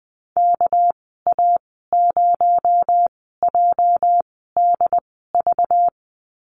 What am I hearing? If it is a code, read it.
Morse code, heard as a keyed tone, about 20 wpm, spaced as for 14 wpm, 700 Hz. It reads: KA0JDV
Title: 245 Lesson 39g - Letter z - Call Signs 20wpm